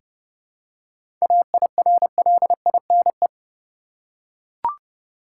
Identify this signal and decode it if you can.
Morse code, heard as a keyed tone, about 32 wpm, 700 Hz. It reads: AIRLINE
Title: Seven Letter Words - 30wpm